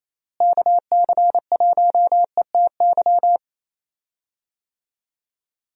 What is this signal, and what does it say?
Morse code, heard as a keyed tone, about 28 wpm, 700 Hz. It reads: KC1ETY
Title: USA Call Signs - 6 Characters - Repeated 3x - 28wpm